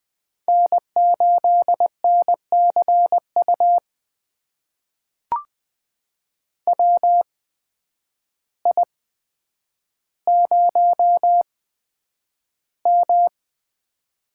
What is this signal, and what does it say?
Morse code, heard as a keyed tone, about 20 wpm, spaced as for 9 wpm, 700 Hz. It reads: N8NCU WI0M
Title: USA Call Signs - 8x Character Spacing - 20wpm